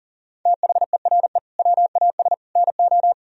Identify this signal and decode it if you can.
Morse code, heard as a keyed tone, about 40 wpm, 700 Hz. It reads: THERE WAS NO
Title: Sentences from Top 1000 Words - Review Part 4 - 40wpm